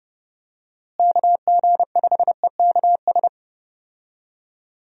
Morse code, heard as a keyed tone, about 30 wpm, 700 Hz. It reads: KG5EKS
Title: USA Call Signs - 6 Characters - Repeated 2x - 30wpm